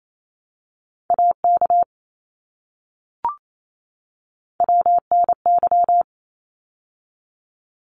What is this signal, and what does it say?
Morse code, heard as a keyed tone, about 28 wpm, 700 Hz. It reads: AK WNY
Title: ARRL Sections 28wpm